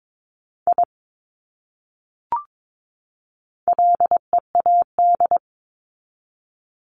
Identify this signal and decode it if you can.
Morse code, heard as a keyed tone, about 22 wpm, 700 Hz. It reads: I LEAD
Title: Top 500 Words 22wpm